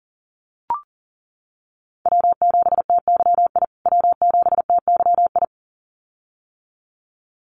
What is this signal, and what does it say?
Morse code, heard as a keyed tone, about 40 wpm, 700 Hz. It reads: W7TYI W7TYI
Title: USA Call Signs Repeated 2x 40wpm